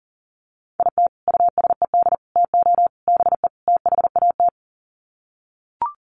Morse code, heard as a keyed tone, about 40 wpm, 700 Hz. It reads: IT USED TO BE THAT
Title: Sets of 5 Words Encore2 40wpm